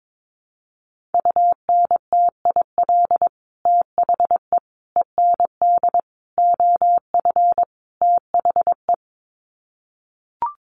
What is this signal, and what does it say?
Morse code, heard as a keyed tone, about 22 wpm, 700 Hz. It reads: UNTIL THE END OF THE
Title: Sets of 5 Words Encore 22wpm